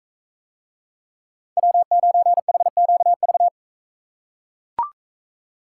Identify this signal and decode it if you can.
Morse code, heard as a keyed tone, about 44 wpm, 700 Hz. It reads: W9HQV